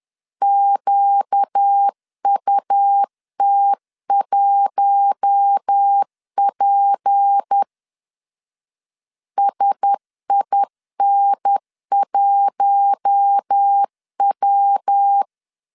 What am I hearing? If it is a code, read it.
Morse code, heard as a keyed tone, about 10 wpm, 800 Hz. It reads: QUT1P SIN1W
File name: LEZIONI / 6 PKX0 / 6 P 10wpm 2432